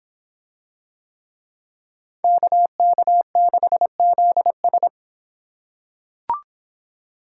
Morse code, heard as a keyed tone, about 24 wpm, 700 Hz. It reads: KK6ZS